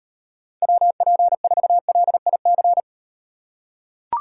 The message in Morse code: WP4LIC

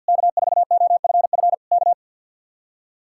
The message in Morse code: K4OFF K

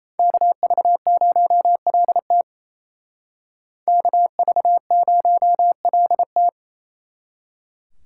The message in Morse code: KV0LT KV0LT

